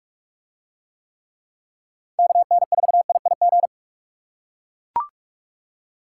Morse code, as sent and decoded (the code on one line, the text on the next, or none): -.- -. ....- .. .. --.
KN4IIG